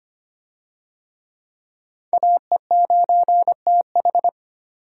Morse code, heard as a keyed tone, about 25 wpm, 700 Hz. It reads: AE9TH
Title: USA Call Signs - 5 Characters - 25wpm